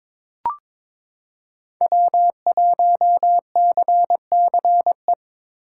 Morse code, heard as a keyed tone, about 22 wpm, 700 Hz. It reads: W1CCE
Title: K1USN SST QSO Practice - 22wpm